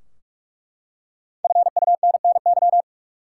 Morse code, heard as a keyed tone, about 41 wpm, 700 Hz. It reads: FUNNY
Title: Five Letter Words - 45wpm